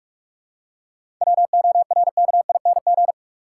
Morse code, heard as a keyed tone, about 46 wpm, 700 Hz. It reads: WORKING
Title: Seven Letter Words - 45wpm